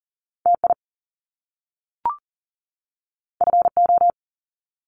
Morse code, heard as a keyed tone, about 40 wpm, 700 Hz. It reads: TI FO
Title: Top Two Letter Combinations in Words 40wpm